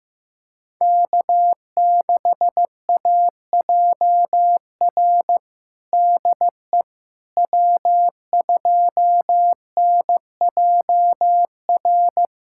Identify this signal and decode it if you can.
Morse code, heard as a keyed tone, about 15 wpm, 700 Hz. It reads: K6AJR DE W2NJR
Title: USA Callsign DE Callsign - 5 Characters - 15wpm